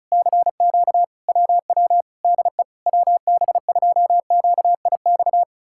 Morse code, heard as a keyed tone, about 40 wpm, 700 Hz. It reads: CQ WW DE WB2QIX